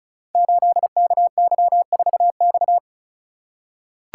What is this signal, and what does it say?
Morse code, heard as a keyed tone, about 36 wpm, 700 Hz. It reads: 8KY4X
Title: Code Groups - 5 Letter:Numbers - 35wpm